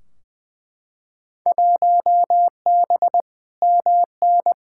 Morse code, heard as a keyed tone, about 20 wpm, 700 Hz. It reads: 1B MN
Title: Field Day Exchanges Encore - 20wpm